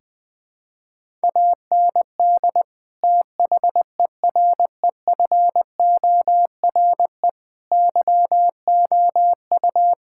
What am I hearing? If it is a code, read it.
Morse code, heard as a keyed tone, about 20 wpm, 700 Hz. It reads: AND THEREFORE YOU
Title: Sentences from Top 1000 Words - Review Part 5 - 20wpm